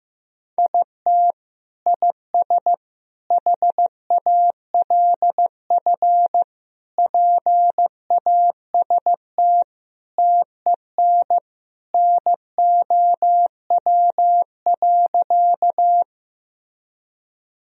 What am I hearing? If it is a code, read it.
Morse code, heard as a keyed tone, about 16 wpm, 700 Hz. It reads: IT IS HALF PAST TEN NOW.
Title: Sentences from Top 1000 Words - Review Part 3 - 15wpm